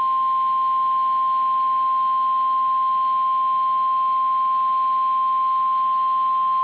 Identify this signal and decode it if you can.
Emergency Alert System (SAME) tones (the 1050 Hz attention tone). It is heard from 1.2 to 5.5 s.
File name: EAS Logs / KMNR